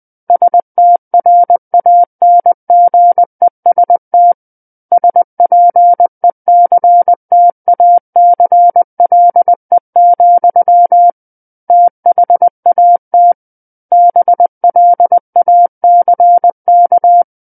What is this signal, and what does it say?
Morse code, heard as a keyed tone, about 20 wpm, 700 Hz. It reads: STRANGEST SPECTACLE, THAT BLACK
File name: 20 WPM